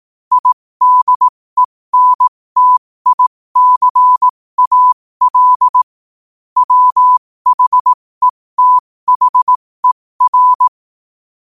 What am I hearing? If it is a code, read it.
Morse code, heard as a keyed tone, about 18 wpm, spaced as for 15 wpm, 1000 Hz. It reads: IDENTICAL WHETHER
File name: qotdfiles / 15wpm